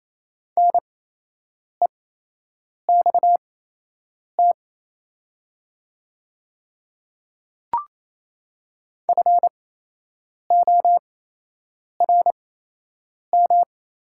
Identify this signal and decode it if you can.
Morse code, heard as a keyed tone, about 28 wpm, spaced as for 8 wpm, 700 Hz. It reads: NEXT FORM
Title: 8x Character Spacing - Top 500 Words - 28wpm